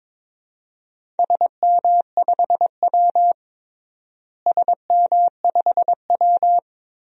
Morse code, heard as a keyed tone, about 22 wpm, 700 Hz. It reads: SM5W SM5W
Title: DXCC-100 Call Signs - 4 Characters - Repeated 3x - 22wpm